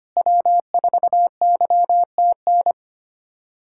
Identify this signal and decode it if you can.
Morse code, heard as a keyed tone, about 25 wpm, 700 Hz. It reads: W4YTN
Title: USA Callsign DE Callsign - 5 Characters - 25wpm